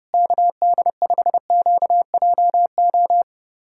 Morse code, heard as a keyed tone, about 30 wpm, 700 Hz. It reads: KD5QJO
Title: USA Callsign DE Callsign 30wpm